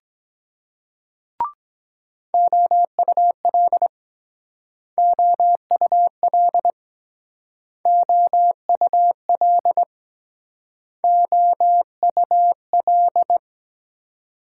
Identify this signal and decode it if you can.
Morse code, heard as a keyed tone, about 21 wpm, 700 Hz. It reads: OUL OUL OUL OUL
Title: Speed-Racing Multi-letter Phonemes - 17wpm